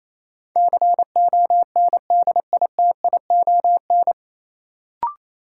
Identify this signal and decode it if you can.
Morse code, heard as a keyed tone, about 28 wpm, 700 Hz. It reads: CONDITION
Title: Sentences from Top 1000 Words - New Word Drill - 28wpm